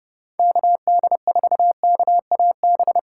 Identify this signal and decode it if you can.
Morse code, heard as a keyed tone, about 30 wpm, 700 Hz. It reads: KD4KAB
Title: USA Call Signs - 6 Characters - 3x Character Spacing - 30wpm